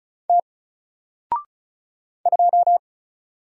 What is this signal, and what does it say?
Morse code, heard as a keyed tone, about 35 wpm, 700 Hz. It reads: T 2
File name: Single-Letter-Number-35wpm